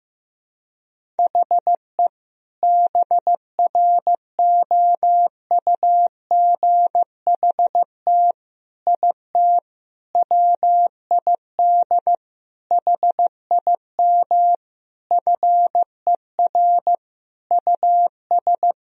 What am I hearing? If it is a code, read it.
Morse code, heard as a keyed tone, about 15 wpm, 700 Hz. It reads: HE BROUGHT IT WID HIM FER US
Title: Sentences from Top 500 Words and Abbreviations 15wpm